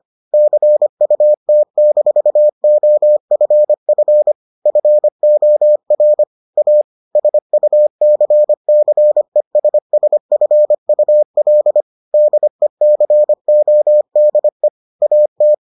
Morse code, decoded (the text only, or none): CUT-OFF FOR A SUCCESSFUL DECODE AT